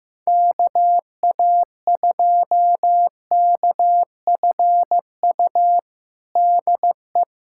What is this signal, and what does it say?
Morse code, heard as a keyed tone, about 15 wpm, 700 Hz. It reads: KA2KFU DE
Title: USA Callsign DE Callsign - 6 Characters - 15wpm